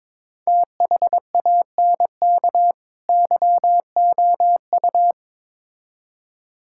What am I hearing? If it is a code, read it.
Morse code, heard as a keyed tone, about 22 wpm, 700 Hz. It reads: THANK YOU
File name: Build A Phrase - 2-Word Phrases - 22wpm